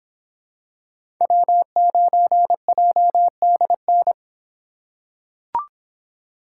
Morse code, heard as a keyed tone, about 26 wpm, 700 Hz. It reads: W9JDN